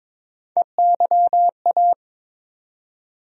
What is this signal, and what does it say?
Morse code, heard as a keyed tone, about 22 wpm, 700 Hz. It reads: EYA
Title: USA Call Sign Postfixes - 3 Letters 22wpm